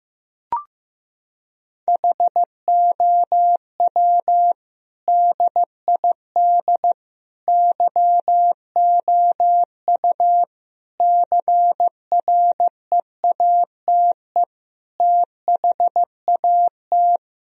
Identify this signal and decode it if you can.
Morse code, heard as a keyed tone, about 16 wpm, 700 Hz. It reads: HOW DID YOU CREATE THAT
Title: Sentences from Top 1000 Words - Review Part 3 - 15wpm